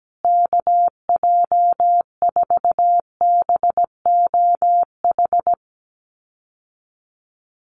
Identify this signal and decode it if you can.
Morse code, heard as a keyed tone, about 17 wpm, 700 Hz. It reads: KJ4BOH
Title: USA Call Signs Repeated 2x 17wpm